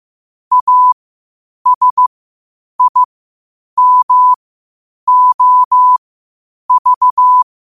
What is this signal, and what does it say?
Morse code, heard as a keyed tone, about 15 wpm, spaced as for 8 wpm, 1000 Hz. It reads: ASIMOV
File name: qotdfiles / 8wpm